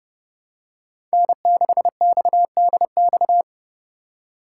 Morse code, heard as a keyed tone, about 29 wpm, 700 Hz. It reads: N6XDX
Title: USA Call Signs - 5 Characters - 4x Character Spacing - 30wpm